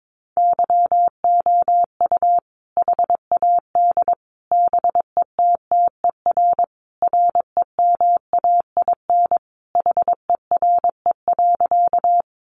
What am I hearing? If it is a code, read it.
Morse code, heard as a keyed tone, about 22 wpm, 700 Hz. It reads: YOU HAD BETTER REMAIN HERE.
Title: Sentences from Top 1200 Words 22wpm